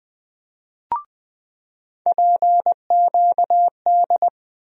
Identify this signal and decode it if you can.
Morse code, heard as a keyed tone, about 20 wpm, 700 Hz. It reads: PQD
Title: USA Call Sign Postfixes - 3 Letters 20wpm